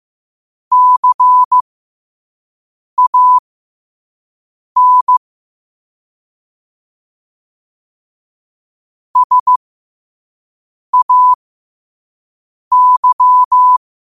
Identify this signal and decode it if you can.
Morse code, heard as a keyed tone, about 15 wpm, spaced as for 5 wpm, 1000 Hz. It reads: CAN SAY
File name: qotdfiles / 5wpm